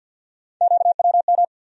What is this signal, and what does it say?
Morse code, heard as a keyed tone, about 51 wpm, 700 Hz. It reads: QWM
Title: USA Call Sign Postfixes - 3 Letters 50wpm